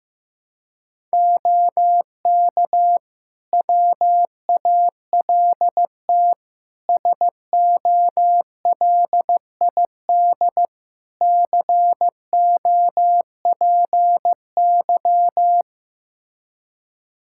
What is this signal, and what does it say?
Morse code, heard as a keyed tone, about 16 wpm, 700 Hz. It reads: OK WALT SOLID COPY
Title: Practice QSO - Exchange 3 - More Info - 15wpm